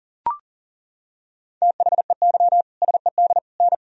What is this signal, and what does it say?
Morse code, heard as a keyed tone, about 40 wpm, 700 Hz. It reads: THEY SED N
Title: Sentences from Top 100 Words and Abbreviations 40wpm